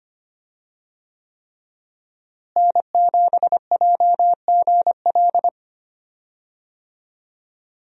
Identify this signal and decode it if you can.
Morse code, heard as a keyed tone, about 25 wpm, 700 Hz. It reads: N7JGL
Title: USA Call Signs - 5 Characters - Repeated 3x - 25wpm